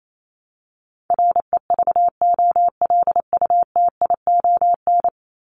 Morse code, heard as a keyed tone, about 28 wpm, 700 Hz. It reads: REVOLUTION
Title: Top 1751-2000 Words 28wpm